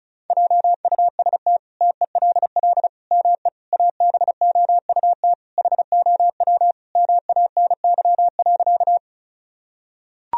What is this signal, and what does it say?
Morse code, heard as a keyed tone, about 34 wpm, 700 Hz. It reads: JUST TELL ME ABOUT HOW MANY.
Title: Sentences from Top 1000 Words - Review Part 1 - 35wpm